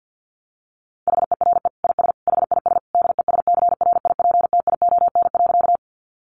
Morse code, heard as a keyed tone, about 50 wpm, 700 Hz. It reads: HERE IS HIS DESCRIPTION.
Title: Sentences from Top 1300 Words 50wpm